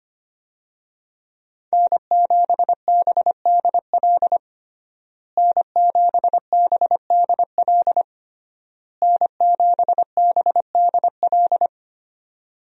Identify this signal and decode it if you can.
Morse code, heard as a keyed tone, about 25 wpm, 700 Hz. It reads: N7BDL N7BDL N7BDL
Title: USA Call Signs - 5 Characters - Repeated 3x - 25wpm